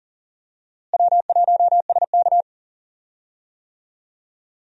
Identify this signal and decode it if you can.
Morse code, heard as a keyed tone, about 40 wpm, 700 Hz. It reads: W1SK
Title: USA Call Signs - 4 Characters - Repeated 3x - 40wpm